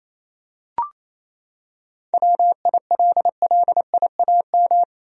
Morse code, heard as a keyed tone, about 28 wpm, 700 Hz. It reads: WILLIAM
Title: USA Common Names - 28wpm